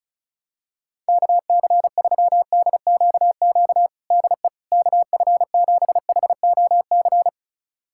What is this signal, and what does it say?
Morse code, heard as a keyed tone, about 36 wpm, 700 Hz. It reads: KC3DQQ DE KF7HOC
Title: USA Callsign DE Callsign 35wpm